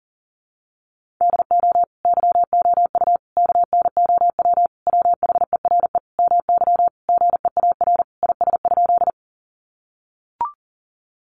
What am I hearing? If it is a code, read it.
Morse code, heard as a keyed tone, about 40 wpm, 700 Hz. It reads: DO YOU KNOW WHERE MY GEAR IS?
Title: Sentences from Top 1400 Words 40wpm